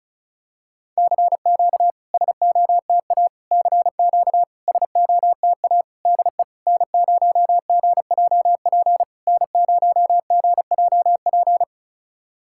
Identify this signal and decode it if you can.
Morse code, heard as a keyed tone, about 34 wpm, 690 Hz. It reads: CQ SOTA CQ SOTA DE N0GJP N0GJP
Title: SOTA QSO Practice 35wpm